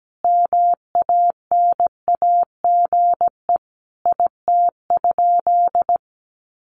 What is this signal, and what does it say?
Morse code, heard as a keyed tone, about 17 wpm, 700 Hz. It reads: MANAGE IT?
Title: Sentences from Top 1400 Words 17wpm